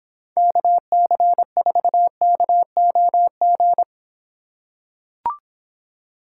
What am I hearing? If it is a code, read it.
Morse code, heard as a keyed tone, about 23 wpm, 700 Hz. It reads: KC4KOG